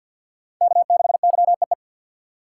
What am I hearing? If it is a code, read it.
Morse code, heard as a keyed tone, about 49 wpm, 700 Hz. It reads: K6YEE
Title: USA Call Signs - 2x Character Spacing - 50wpm